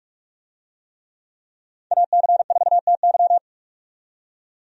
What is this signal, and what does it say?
Morse code, heard as a keyed tone, about 46 wpm, 700 Hz. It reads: AC4TY
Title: USA Call Signs - 5 Characters - 45wpm